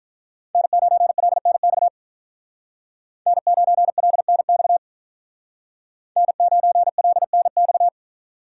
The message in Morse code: N9LNX N9LNX N9LNX

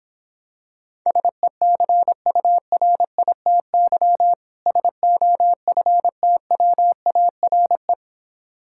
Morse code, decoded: SECURITY SOFTWARE

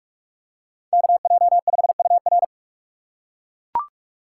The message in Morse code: KJ5UR

